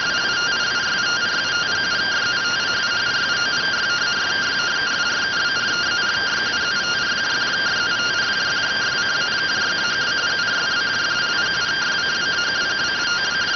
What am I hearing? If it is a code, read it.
Two-tone teleprinter signal, about 170 Hz shift.